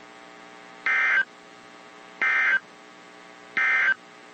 Emergency Alert System (SAME) tones, heard as harsh buzzing bursts. It is heard from 0.7 to 4.0 s.
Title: WZ2523 - NOAA Weather Radio